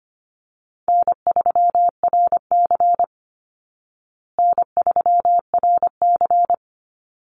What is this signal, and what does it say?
Morse code, heard as a keyed tone, about 25 wpm, 700 Hz. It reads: N3RC N3RC
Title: Call Signs Repeated 2x - 25wpm